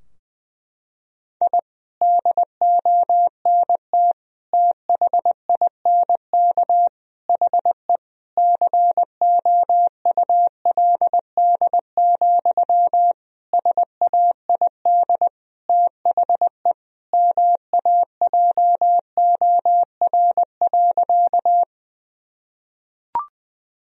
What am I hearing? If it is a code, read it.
Morse code, heard as a keyed tone, about 20 wpm, 700 Hz. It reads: I DONT THINK HE COULD, SAID THE MAJOR.
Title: Sentences from Top 1000 Words - Review Part 4 - 20wpm